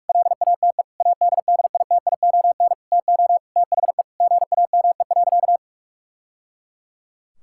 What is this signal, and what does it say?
Morse code, heard as a keyed tone, about 41 wpm, 700 Hz. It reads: LATE ADDITION TO THE GAME.
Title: Sentences from Top 1000 Words - Review Part 6 - 45wpm